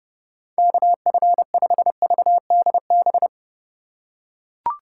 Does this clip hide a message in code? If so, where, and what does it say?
Yes: Morse code from 0.6 to 3.3 s, heard as a keyed tone, about 28 wpm, 700 Hz. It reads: KF5VDB